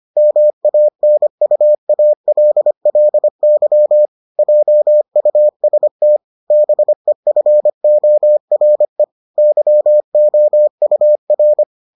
Morse code, heard as a keyed tone, about 25 wpm, 600 Hz. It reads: MANUALLY JUST BEFORE YOUR